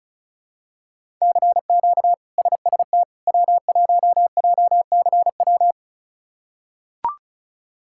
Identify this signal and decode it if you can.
Morse code, heard as a keyed tone, about 36 wpm, 700 Hz. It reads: CQ SST W1JCW
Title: K1USN SST QSO Practice - 35wpm